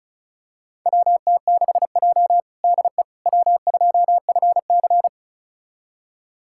Morse code, heard as a keyed tone, about 36 wpm, 700 Hz. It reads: WT6J DE W2FC
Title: USA Callsign DE Callsign - 4 Characters - 35wpm